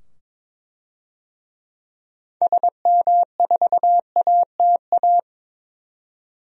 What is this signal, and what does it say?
Morse code, heard as a keyed tone, about 21 wpm, 700 Hz. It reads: SM4ATA
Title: DXCC-100 Call Signs - 6 Characters - 22wpm